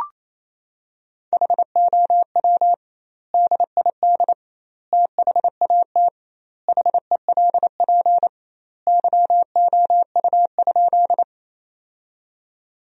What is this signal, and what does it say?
Morse code, heard as a keyed tone, about 28 wpm, 700 Hz. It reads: HOW DID THAT HELP YOU?
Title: 2x Word-Spacing - Sentences from Top 200 Words - 28wpm